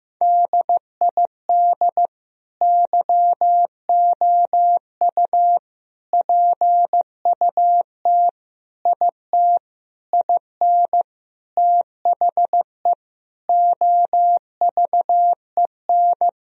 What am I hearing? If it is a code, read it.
Morse code, heard as a keyed tone, about 15 wpm, 700 Hz. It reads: DID YOU PUT IT IN THE OVEN
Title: Sentences from Top 1000 Words - Review Part 4 - 15wpm